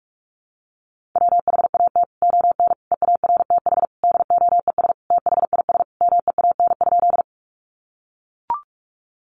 Morse code, heard as a keyed tone, about 44 wpm, 700 Hz. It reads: WHAT ON EARTH DOES THIS MEAN?
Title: Sentences from Top 200 Words 45wpm